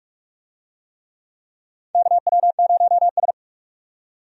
Morse code, heard as a keyed tone, about 46 wpm, 700 Hz. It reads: KW0S
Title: USA Call Signs - 4 Characters - 45wpm